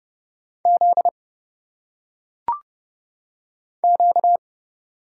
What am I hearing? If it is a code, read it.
Morse code, heard as a keyed tone, about 30 wpm, 700 Hz. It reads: Z Q
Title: Single Letters - Phonetic Pronunciation 30wpm